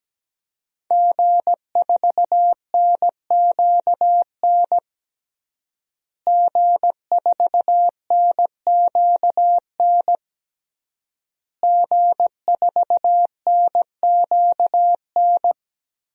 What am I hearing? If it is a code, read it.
Morse code, heard as a keyed tone, about 17 wpm, 700 Hz. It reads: G4NQN G4NQN G4NQN
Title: DXCC-100 Call Signs - 5 Characters - Repeated 3x - 17wpm